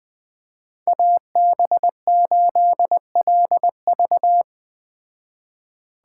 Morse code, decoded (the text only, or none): AB8LV